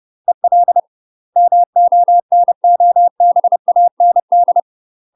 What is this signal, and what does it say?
Morse code, heard as a keyed tone, about 30 wpm, 700 Hz. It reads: EL MONOBAND
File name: UK MASTER QSOs TRAINING / 30WPM / UK MASTER QSOs 0044